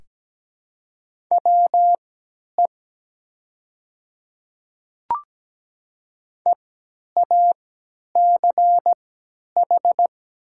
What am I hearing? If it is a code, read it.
Morse code, heard as a keyed tone, about 17 wpm, spaced as for 10 wpm, 700 Hz. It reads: WE EACH
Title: 3x Character Spacing - Top 200 Words - 17wpm